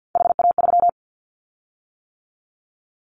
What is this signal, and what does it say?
Morse code, heard as a keyed tone, about 50 wpm, 700 Hz. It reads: HA3
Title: Callsign Prefixes for Top 25 DXCC - 50wpm